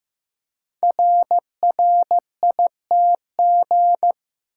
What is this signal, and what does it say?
Morse code, heard as a keyed tone, about 15 wpm, 700 Hz. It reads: RRITG
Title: Code Groups - 5 Letter:Numbers - 15wpm